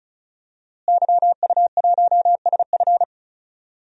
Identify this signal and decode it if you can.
Morse code, heard as a keyed tone, about 36 wpm, 700 Hz. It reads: YU1SF
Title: DXCC-100 Call Signs - 5 Characters - Repeated 2x - 35wpm